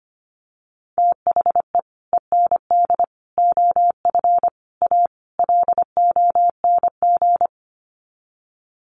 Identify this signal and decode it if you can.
Morse code, heard as a keyed tone, about 25 wpm, 700 Hz. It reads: THE END OF A LONG
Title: Sets of 5 Words Encore2 25wpm